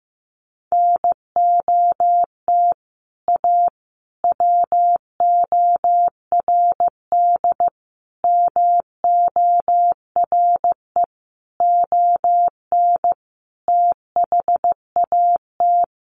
Morse code, heard as a keyed tone, about 15 wpm, 700 Hz. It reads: NOT A WORD MORE ON THAT
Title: Sentences from Top 800 Words 15wpm